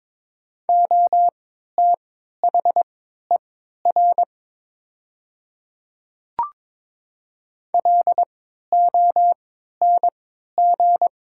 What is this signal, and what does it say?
Morse code, heard as a keyed tone, about 22 wpm, spaced as for 13 wpm, 700 Hz. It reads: OTHER LONG
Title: 3x Character Spacing - Top 500 Words - 22wpm